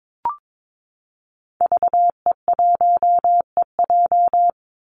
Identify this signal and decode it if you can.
Morse code, heard as a keyed tone, about 22 wpm, 700 Hz. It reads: VE1EJ
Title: Call Signs Repeated 3x - 22wpm